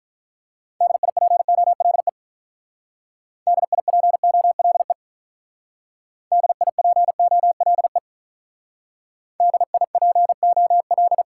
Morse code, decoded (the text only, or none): DIPOLE DIPOLE DIPOLE DIPOL